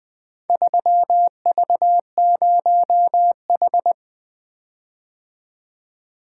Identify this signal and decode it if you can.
Morse code, heard as a keyed tone, about 20 wpm, 700 Hz. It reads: 3V0H
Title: Four Letter-Number Combinations - 20wpm